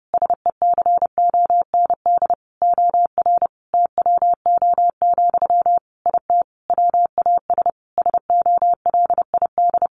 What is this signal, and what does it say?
Morse code, heard as a keyed tone, about 30 wpm, 700 Hz. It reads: SECOND OR TWO, IT WAS SOLID
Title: Sentences from Top 1500 Words 30wpm